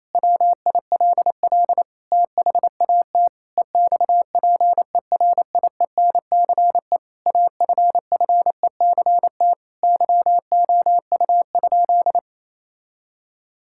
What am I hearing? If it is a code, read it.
Morse code, heard as a keyed tone, about 28 wpm, 700 Hz. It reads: WILL THAT EXPERIENCE AFFECT YOU?
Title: Sentences from Top 1000 Words - Review Part 6 - 28wpm